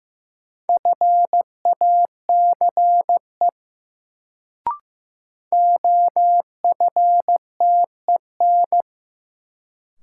Morse code, heard as a keyed tone, about 15 wpm, 700 Hz. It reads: FACE OFTEN
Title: Sentences from Top 300 Words - New Word Drill - 15wpm